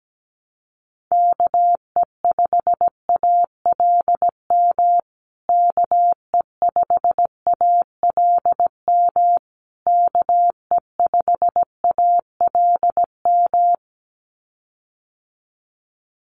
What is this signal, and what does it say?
Morse code, heard as a keyed tone, about 17 wpm, 700 Hz. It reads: KE5ALM KE5ALM KE5ALM
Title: USA Call Signs Repeated 3x 17wpm